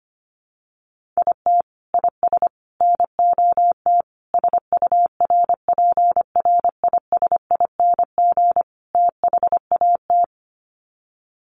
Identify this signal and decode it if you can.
Morse code, heard as a keyed tone, about 25 wpm, 700 Hz. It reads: IT IS NOT SURPRISING THAT
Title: Sets of 5 Words 25wpm